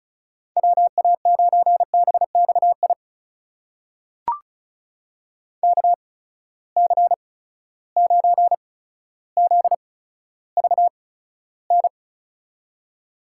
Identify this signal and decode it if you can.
Morse code, heard as a keyed tone, about 36 wpm, spaced as for 15 wpm, 700 Hz. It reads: WA9BXI KC9ZVN
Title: USA Call Signs - 6 Characters - 8x Character Spacing - 35wpm